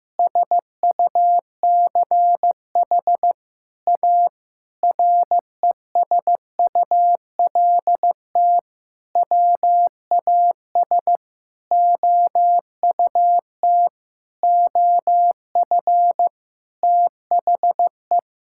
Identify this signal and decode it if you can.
Morse code, heard as a keyed tone, about 15 wpm, 700 Hz. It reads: SUCH A RESULT WAS OUT OF THE
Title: Sentences from Top 1000 Words - Review Part 5 - 15wpm